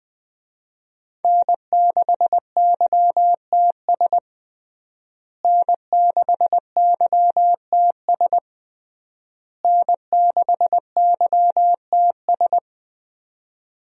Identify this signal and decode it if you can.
Morse code, heard as a keyed tone, about 21 wpm, 700 Hz. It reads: N6YTS N6YTS N6YTS
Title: USA Call Signs - 5 Characters - Repeated 3x - 20wpm